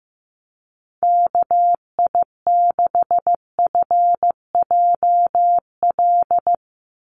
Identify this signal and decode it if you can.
Morse code, heard as a keyed tone, about 15 wpm, 700 Hz. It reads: KI6FJL
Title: USA Call Signs 15wpm